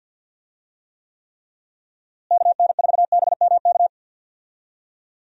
Morse code, heard as a keyed tone, about 51 wpm, 700 Hz. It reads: KN4BMK
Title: USA Call Signs - 6 Characters - Repeated 3x - 50wpm